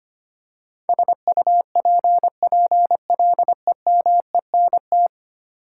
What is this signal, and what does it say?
Morse code, heard as a keyed tone, about 25 wpm, 700 Hz. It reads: SUPPLEMENT
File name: Ten Letter Words - 25wpm